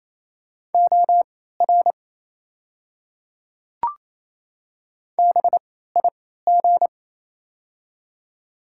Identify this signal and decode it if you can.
Morse code, heard as a keyed tone, about 27 wpm, spaced as for 16 wpm, 690 Hz. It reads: OR BIG
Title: 3x Character Spacing - Top 300 Words - 28wpm